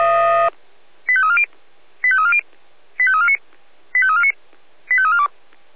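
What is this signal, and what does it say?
Five-tone selective call (ZVEI), heard as quick runs of five tones, tones of about 70 ms. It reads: 85340 85340 85349 85349 85342
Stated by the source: ZVEI five-tone call 85275 received by SDR.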